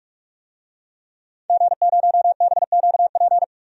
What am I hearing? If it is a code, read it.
Morse code, heard as a keyed tone, about 46 wpm, 700 Hz. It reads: G0BQP
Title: DXCC-100 Call Signs - 5 Characters - Repeated 3x - 45wpm